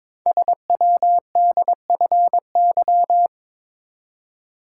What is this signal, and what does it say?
Morse code, heard as a keyed tone, about 22 wpm, 700 Hz. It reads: SWDFY
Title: Code Groups - 5 Letters - 22wpm